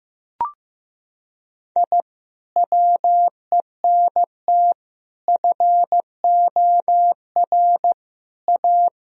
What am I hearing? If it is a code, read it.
Morse code, heard as a keyed tone, about 15 wpm, 700 Hz. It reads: I WENT FOR A
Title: Sentences from Top 1000 Words - Review Part 1 - 15wpm